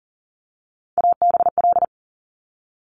Morse code, heard as a keyed tone, about 38 wpm, 690 Hz. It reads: ABL
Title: Top Three Letter Combinations in Words 40wpm